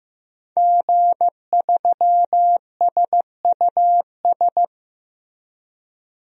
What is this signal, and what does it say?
Morse code, heard as a keyed tone, about 15 wpm, 700 Hz. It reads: G3SUS
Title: DXCC-100 Call Signs - 5 Characters - Repeated 2x - 15wpm